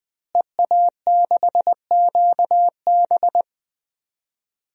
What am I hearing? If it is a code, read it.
Morse code, heard as a keyed tone, about 20 wpm, 700 Hz. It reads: EA6QB
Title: DXCC-100 Call Signs - 5 Characters - Repeated 2x - 20wpm